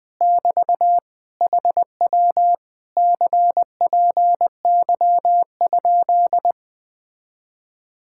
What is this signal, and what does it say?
Morse code, heard as a keyed tone, about 20 wpm, 700 Hz. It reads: <BT> HW CPY?
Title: Speed-Racing Common QSO Phrases - 17wpm